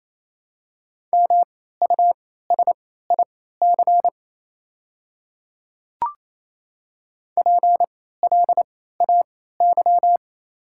Morse code, heard as a keyed tone, about 28 wpm, spaced as for 16 wpm, 700 Hz. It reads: MUSIC PLAY
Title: 3x Character Spacing - Top 300 Words - 28wpm